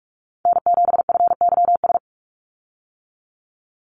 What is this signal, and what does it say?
Morse code, heard as a keyed tone, about 44 wpm, 700 Hz. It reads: N7FYS
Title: USA Call Signs 45wpm